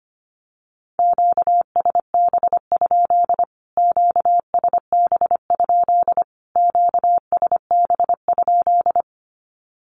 Morse code, heard as a keyed tone, about 25 wpm, 700 Hz. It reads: QSB? QSB? QSB?
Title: Q-Codes Repeated 3x 25wpm